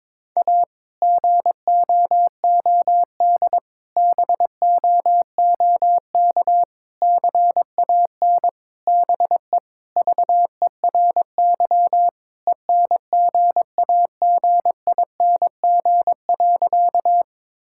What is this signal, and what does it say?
Morse code, heard as a keyed tone, about 23 wpm, 700 Hz. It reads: A GOOD BOOK CAN BE VERY ENGAGING.
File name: practice / 22wpm / Sentences with 2-Word Phrases - 22wpm